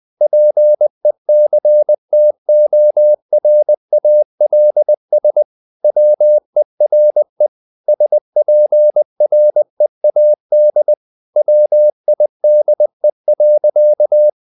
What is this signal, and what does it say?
Morse code, heard as a keyed tone, about 20 wpm, 600 Hz. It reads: PECTORALS WERE SPREAD WIDE.
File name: the old man and the sea part 2 20wpm